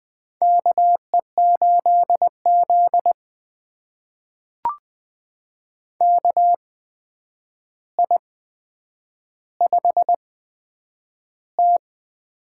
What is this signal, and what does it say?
Morse code, heard as a keyed tone, about 20 wpm, spaced as for 9 wpm, 700 Hz. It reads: KE8Z KI5T
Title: USA Call Signs - 8x Character Spacing - 20wpm